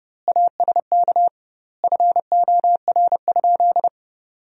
Morse code, heard as a keyed tone, about 32 wpm, 700 Hz. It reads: ASK FOR?
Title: 2x Word-Spacing - Sentences from Top 200 Words - 30wpm